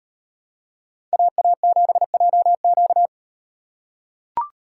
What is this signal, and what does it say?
Morse code, heard as a keyed tone, about 38 wpm, 700 Hz. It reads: AA7JQ